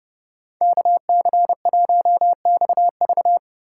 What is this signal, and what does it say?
Morse code, heard as a keyed tone, about 30 wpm, 700 Hz. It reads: KC1XV